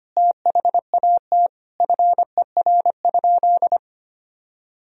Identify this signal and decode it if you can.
Morse code, heard as a keyed tone, about 25 wpm, 700 Hz. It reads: THAT FER?
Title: Sentences from Top 200 Words and Abbreviations 25wpm